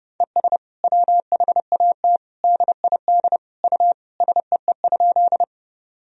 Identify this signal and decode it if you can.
Morse code, heard as a keyed tone, about 30 wpm, 700 Hz. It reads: ES WHAT DID U SEE?
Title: Sentences from Top 100 Words and Abbreviations 30wpm